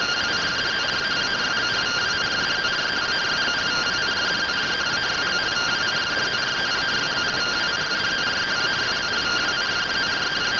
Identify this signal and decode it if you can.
Two-tone teleprinter signal, about 170 Hz shift.